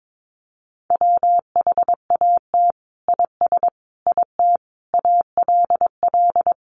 Morse code, heard as a keyed tone, about 22 wpm, 700 Hz. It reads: WHAT IS IT ALL
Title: Sentences from Top 100 Words 22wpm